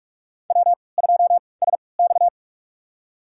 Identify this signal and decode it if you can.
Morse code, heard as a keyed tone, about 47 wpm, spaced as for 26 wpm, 700 Hz. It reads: W2SX
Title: USA Call Signs - 4 Characters - 3x Character Spacing - 45wpm